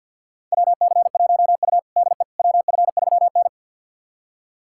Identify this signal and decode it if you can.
Morse code, heard as a keyed tone, about 49 wpm, 700 Hz. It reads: WC1U DE WF3N
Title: USA Callsign DE Callsign - 4 Characters - 50wpm